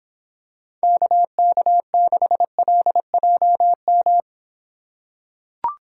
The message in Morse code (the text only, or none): KK6LJM